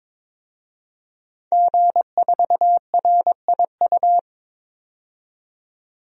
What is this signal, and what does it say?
Morse code, heard as a keyed tone, about 22 wpm, 700 Hz. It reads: G4RIU
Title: DXCC-100 Call Signs - 5 Characters - 22wpm